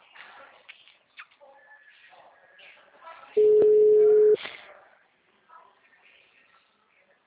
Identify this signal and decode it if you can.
Ringback tone (Europe, 425 Hz), with speech.